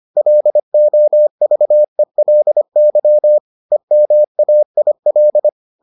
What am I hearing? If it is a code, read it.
Morse code, heard as a keyed tone, about 25 wpm, 600 Hz. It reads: LOVELY EMAIL